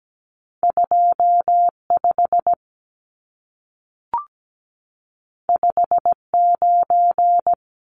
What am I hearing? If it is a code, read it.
Morse code, heard as a keyed tone, about 17 wpm, 700 Hz. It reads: 25 59
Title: Numbers - 2-Digit 17wpm